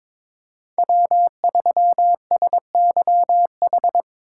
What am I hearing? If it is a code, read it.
Morse code, heard as a keyed tone, about 25 wpm, 700 Hz. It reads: W3SYH
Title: Speed Racing - USA 1x3 Call Signs 22wpm